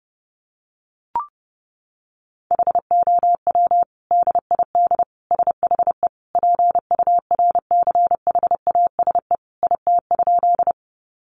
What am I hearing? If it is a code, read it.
Morse code, heard as a keyed tone, about 30 wpm, 700 Hz. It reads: HOW DID SHE PURCHASE IT?
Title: Sentences from Top 800 Words 30wpm